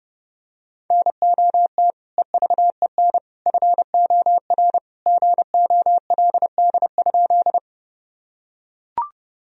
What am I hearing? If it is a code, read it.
Morse code, heard as a keyed tone, about 30 wpm, 700 Hz. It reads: NOT EVEN FOR GOLD?
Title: Sentences from Top 1000 Words - Review Part 3 - 30wpm